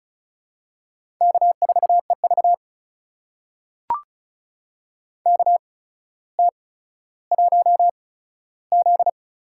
Morse code, heard as a keyed tone, about 36 wpm, spaced as for 15 wpm, 700 Hz. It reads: K4EV KT1Z
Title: USA Call Signs - 4 Characters - 8x Character Spacing - 35wpm